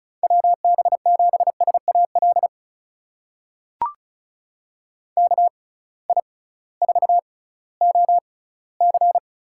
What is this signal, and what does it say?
Morse code, heard as a keyed tone, about 36 wpm, spaced as for 19 wpm, 700 Hz. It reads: WB7SAL KI4OC
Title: USA Call Signs - 6x Character Spacing - 35wpm